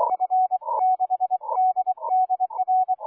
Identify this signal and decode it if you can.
Morse code, heard as a keyed tone, about 25 wpm, 750 Hz. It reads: F6DDR